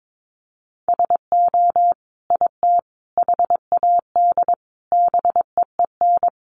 Morse code, heard as a keyed tone, about 22 wpm, 700 Hz. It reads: SO IT HAD BEEN
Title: Sentences from Top 1200 Words 22wpm